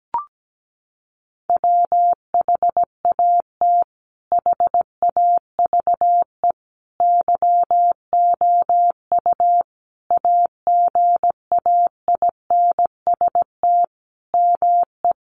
Morse code, heard as a keyed tone, about 17 wpm, 700 Hz. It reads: WHAT HAVE YOU AGAINST ME
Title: Sentences from Top 400 Words 17wpm